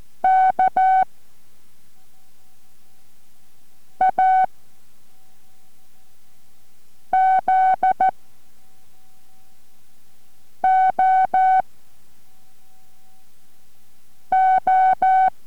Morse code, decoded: K A Z O O